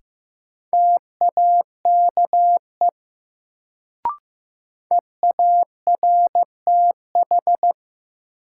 Morse code, heard as a keyed tone, about 15 wpm, 700 Hz. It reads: TAKE EARTH
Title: Sentences from Top 200 Words - New Word Drill - 15wpm